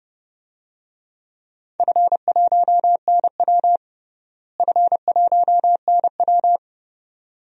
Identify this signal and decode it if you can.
Morse code, heard as a keyed tone, about 30 wpm, 700 Hz. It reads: F1NW F1NW
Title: DXCC-100 Call Signs - 4 Characters - Repeated 2x - 30wpm